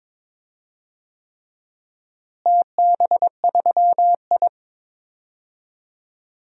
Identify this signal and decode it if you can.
Morse code, heard as a keyed tone, about 22 wpm, 700 Hz. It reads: TB3I
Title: DXCC-100 Call Signs - 4 Characters - Repeated 3x - 22wpm